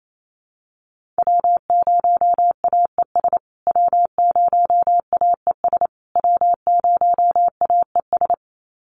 Morse code, heard as a keyed tone, about 28 wpm, 700 Hz. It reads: W0AES W0AES W0AES
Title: USA Call Signs Repeated 3x 28wpm